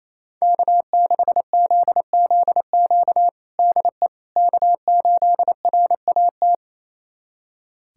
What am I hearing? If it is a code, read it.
Morse code, heard as a keyed tone, about 28 wpm, 700 Hz. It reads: K6ZZQ DE K8RAT
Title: USA Callsign DE Callsign 28wpm